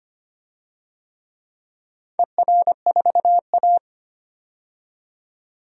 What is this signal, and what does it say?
Morse code, heard as a keyed tone, about 25 wpm, 700 Hz. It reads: ER4A
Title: DXCC-100 Call Signs - 4 Characters - 25wpm